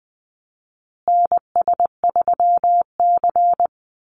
Morse code, heard as a keyed tone, about 20 wpm, 700 Hz. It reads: NS3C
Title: Call Signs Repeated 2x - 20wpm